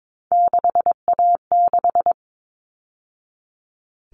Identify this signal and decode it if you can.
Morse code, heard as a keyed tone, about 22 wpm, 700 Hz. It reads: 6A6
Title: All Cut Numbers - 3-Digit 22wpm